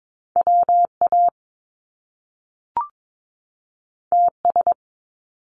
Morse code, heard as a keyed tone, about 22 wpm, 700 Hz. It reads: WA TS
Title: Top Two Letter Combinations in Words 22wpm